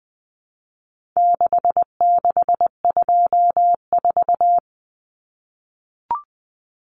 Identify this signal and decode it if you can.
Morse code, heard as a keyed tone, about 20 wpm, 700 Hz. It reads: 6624
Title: Numbers - 4-Digit 20wpm